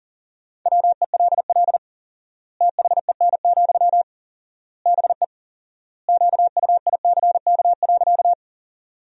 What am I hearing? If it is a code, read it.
Morse code, heard as a keyed tone, about 40 wpm, 700 Hz. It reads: WELL THEN, BE QUICK.
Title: 4x Word-Spacing - Sentences from Top 500 Words - 40wpm